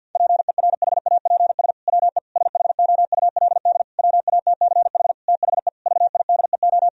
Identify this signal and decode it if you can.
Morse code, heard as a keyed tone, about 51 wpm, 700 Hz. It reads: PERHAPS WE SHOULD WATCH THE VIDEO
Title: Sentences from Top 1000 Words - Review Part 4 - 50wpm